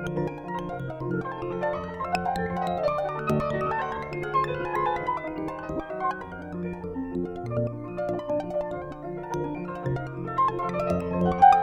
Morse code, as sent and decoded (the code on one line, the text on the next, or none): none